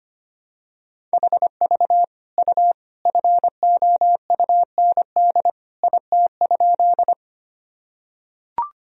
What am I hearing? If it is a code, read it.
Morse code, heard as a keyed tone, about 25 wpm, 700 Hz. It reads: HV U FOUND IT?
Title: Sentences from Top 300 Words and Abbreviations 25wpm